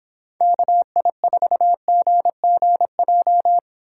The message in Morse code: KI4GGJ

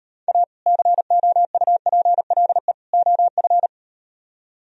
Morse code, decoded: A COUPLE OF

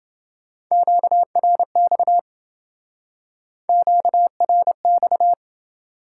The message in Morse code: QRX QRX

